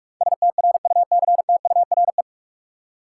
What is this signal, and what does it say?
Morse code, heard as a keyed tone, about 42 wpm, 700 Hz. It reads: STRUCTURE
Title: Sentences from Top 1500 Words - New Word Drill Alt - 45wpm